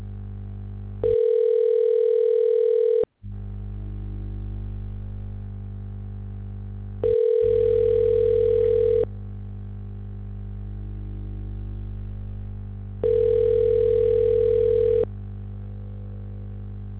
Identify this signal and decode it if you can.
Ringback tone (North America), 440 + 480 Hz.